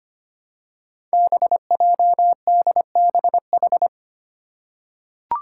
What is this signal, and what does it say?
Morse code, heard as a keyed tone, about 25 wpm, 700 Hz. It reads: BJDBH
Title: Five Challenging Letter-Number Combinations 25wpm